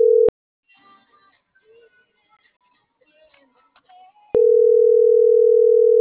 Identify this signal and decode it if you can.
Ringback tone (North America), 440 + 480 Hz.